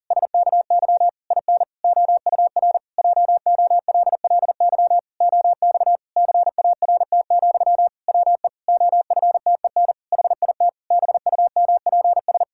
Morse code, heard as a keyed tone, about 40 wpm, 700 Hz. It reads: SKY IN OUR JOLLY OX CART, WE OFTEN HIT BUMPS
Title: Pangram Challenge v2 - 40wpm